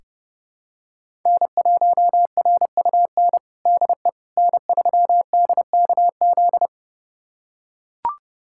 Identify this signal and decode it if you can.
Morse code, heard as a keyed tone, about 30 wpm, 700 Hz. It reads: N1RUN DE N3DKZ
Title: USA Callsign DE Callsign - 5 Characters - 30wpm